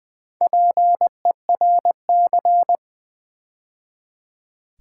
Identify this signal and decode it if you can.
Morse code, heard as a keyed tone, about 20 wpm, 700 Hz. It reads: PERC
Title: Build A Word - 2 Letters At A Time - 8-Letter Words - 20wpm